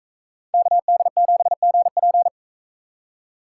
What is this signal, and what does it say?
Morse code, heard as a keyed tone, about 41 wpm, 700 Hz. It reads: KD7GP